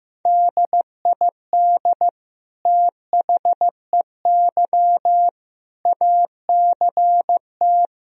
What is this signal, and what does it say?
Morse code, heard as a keyed tone, about 16 wpm, 700 Hz. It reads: DID THEY ACT
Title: Sentences from Top 1000 Words - Review Part 4 - 15wpm